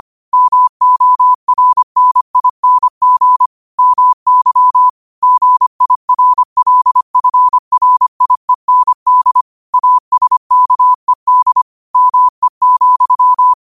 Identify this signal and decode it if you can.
Morse code, heard as a keyed tone, about 25 wpm, 1000 Hz. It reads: MORNING MY GIRLFRIEND ASKED ME,
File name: qotdfiles / 25wpm